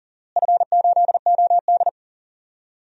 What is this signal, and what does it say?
Morse code, heard as a keyed tone, about 40 wpm, 690 Hz. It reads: F8OD
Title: DXCC-100 Call Signs - 4 Characters - Repeated 2x - 40wpm